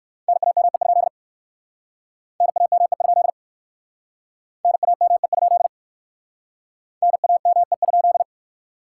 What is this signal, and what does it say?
Morse code, heard as a keyed tone, about 60 wpm, 700 Hz. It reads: NAME? NAME? NAME? NAME?